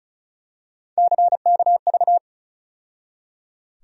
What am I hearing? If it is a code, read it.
Morse code, heard as a keyed tone, about 36 wpm, 700 Hz. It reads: CKV
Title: USA Call Sign Postfixes - 3 Letters 35wpm